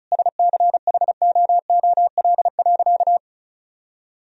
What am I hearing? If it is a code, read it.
Morse code, heard as a keyed tone, about 34 wpm, 700 Hz. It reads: SCHOOL.
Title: Sentences from Top 1000 Words - Review Part 1 - 35wpm